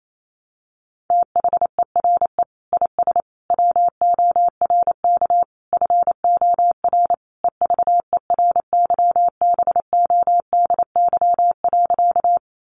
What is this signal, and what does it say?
Morse code, heard as a keyed tone, about 28 wpm, 700 Hz. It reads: THERE IS WORK FOR EVERYBODY.
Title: Sentences from Top 1400 Words 28wpm